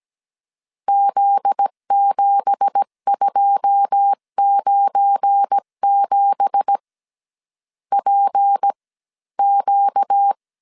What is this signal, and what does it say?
Morse code, heard as a keyed tone, about 17 wpm, spaced as for 14 wpm, 800 Hz. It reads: Z7297 PQ
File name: CW / DET / 1 9 1500C S B 16wpm 2856